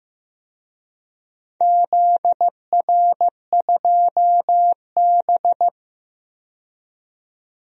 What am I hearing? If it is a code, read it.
Morse code, heard as a keyed tone, about 15 wpm, 700 Hz. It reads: ZR2B
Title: DXCC-100 Call Signs - 4 Characters - 15wpm